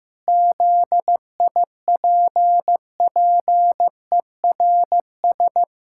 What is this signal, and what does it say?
Morse code, heard as a keyed tone, about 15 wpm, 700 Hz. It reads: ZIPPERS
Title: Pangram Challenge v2 - 15wpm